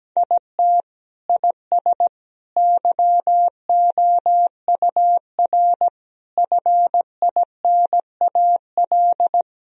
Morse code, heard as a keyed tone, about 17 wpm, 700 Hz. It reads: IT IS YOUR FINAL
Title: Sentences from Top 1000 Words - Review Part 2 - 17wpm